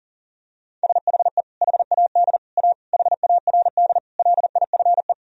Morse code, heard as a keyed tone, about 37 wpm, 700 Hz. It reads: SHE HAD A HARD LIFE
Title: Sentences from Top 1000 Words - Review Part 2 - 40wpm